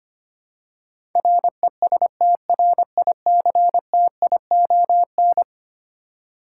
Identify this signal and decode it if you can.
Morse code, heard as a keyed tone, about 25 wpm, 700 Hz. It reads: RESTRICTION
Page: morse-code-ninja-practice-25wpm